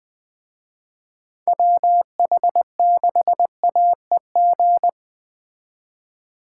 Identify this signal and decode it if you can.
Morse code, heard as a keyed tone, about 20 wpm, 700 Hz. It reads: WH6AEG
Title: Speed Racing - USA 2x3 Call Signs 17wpm